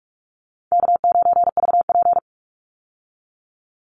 Morse code, heard as a keyed tone, about 44 wpm, 700 Hz. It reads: K9VP
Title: Call Signs - 45wpm